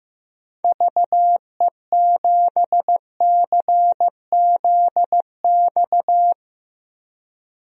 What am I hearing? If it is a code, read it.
Morse code, heard as a keyed tone, about 15 wpm, 700 Hz. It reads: VE7CZX
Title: DXCC-100 Call Signs - 6 Characters - Repeated 2x - 15wpm